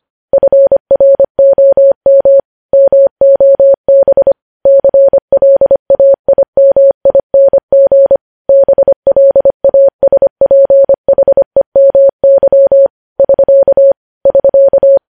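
Morse code, heard as a keyed tone, about 25 wpm, 550 Hz. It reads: FROM MOB CLAIMING BLASPHEMY <SK> <SK>
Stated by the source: Morse code news headlines at 25 wpm.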